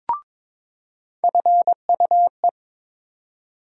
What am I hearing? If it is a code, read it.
Morse code, heard as a keyed tone, about 22 wpm, 700 Hz. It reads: FUE
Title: USA Call Sign Postfixes - 3 Letters 22wpm